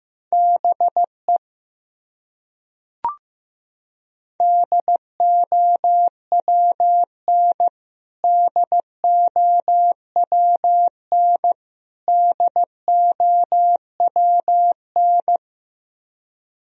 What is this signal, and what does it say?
Morse code, heard as a keyed tone, about 15 wpm, 700 Hz. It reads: BE DOWN DOWN DOWN
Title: Top 100 Words Repeated 15wpm